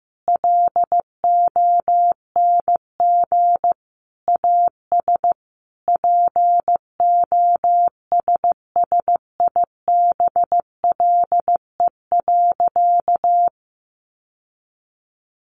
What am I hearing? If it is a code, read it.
Morse code, heard as a keyed tone, about 14 wpm, 700 Hz. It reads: LONG AS POSSIBLE.
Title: Sentences from Top 900 Words 15wpm